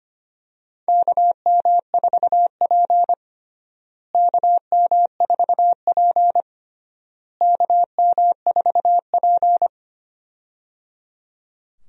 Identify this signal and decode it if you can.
Morse code, heard as a keyed tone, about 25 wpm, 700 Hz. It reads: KM4P KM4P KM4P
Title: USA Call Signs - 4 Characters - Repeated 3x - 25wpm